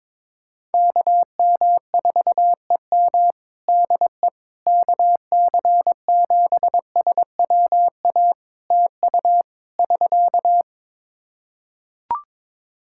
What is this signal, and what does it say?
Morse code, heard as a keyed tone, about 22 wpm, 700 Hz. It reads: KM4EM DE KC7SWA TU <SK>
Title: Practice QSO - Exchange 4 - Ending - 22wpm